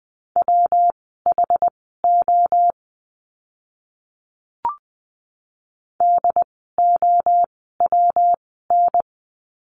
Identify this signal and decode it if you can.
Morse code, heard as a keyed tone, about 20 wpm, spaced as for 14 wpm, 700 Hz. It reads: WHO DOWN
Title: 192 Lesson 31f - Letter v - Words 20wpm